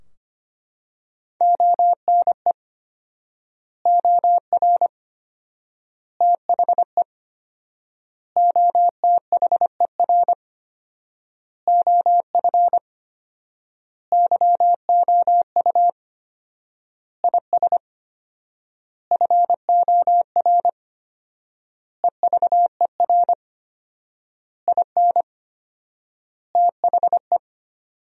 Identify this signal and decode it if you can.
Morse code, heard as a keyed tone, about 25 wpm, 700 Hz. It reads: ONE OR THE OTHER OF YOU IS FOR EVER IN THE
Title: 4x Word-Spacing - Sentences from Top 400 Words - 25wpm